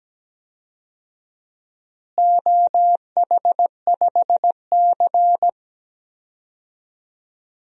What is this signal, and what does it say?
Morse code, heard as a keyed tone, about 17 wpm, 700 Hz. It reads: OH5C
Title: DXCC-100 Call Signs - 4 Characters - Repeated 3x - 17wpm